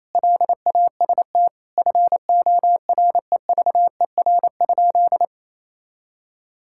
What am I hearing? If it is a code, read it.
Morse code, heard as a keyed tone, about 28 wpm, 700 Hz. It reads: LAST FOREVER?
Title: Sentences from Top 1600 Words 28wpm